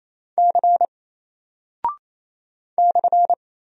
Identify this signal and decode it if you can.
Morse code, heard as a keyed tone, about 28 wpm, 700 Hz. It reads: C /
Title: Single Character (Letter:Number:Punctuation:Prosign) Alt 28wpm